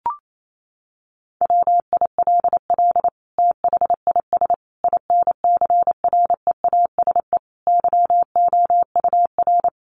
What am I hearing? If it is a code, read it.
Morse code, heard as a keyed tone, about 28 wpm, 700 Hz. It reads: WILL THIS INCREASE YOUR
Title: Sentences from Top 1200 Words 28wpm